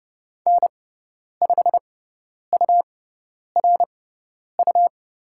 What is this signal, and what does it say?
Morse code, heard as a keyed tone, about 30 wpm, spaced as for 10 wpm, 700 Hz. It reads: N5URU
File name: Lesson 17g - Letter u - Call Signs 30wpm